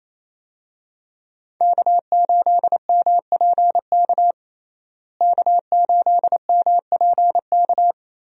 Morse code, heard as a keyed tone, about 28 wpm, 700 Hz. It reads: K8MPK K8MPK
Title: USA Call Signs - 5 Characters - Repeated 2x - 28wpm